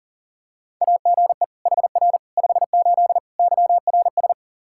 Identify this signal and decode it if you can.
Morse code, heard as a keyed tone, about 40 wpm, 700 Hz. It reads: AGE HR 58 YRS
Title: Practice QSO - Exchange 3 - More Info - 40wpm